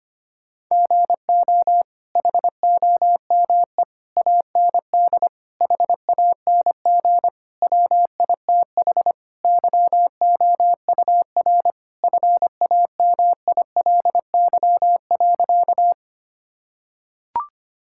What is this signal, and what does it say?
Morse code, heard as a keyed tone, about 25 wpm, 700 Hz. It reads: GO HOME AND HANG WITH YOUR FAMILY.
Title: Sentences from Top 1600 Words 25wpm